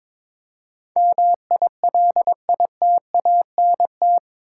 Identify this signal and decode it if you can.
Morse code, heard as a keyed tone, about 21 wpm, 700 Hz. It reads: MILITANT
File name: practice / 22wpm / Eight Letter Words - 22wpm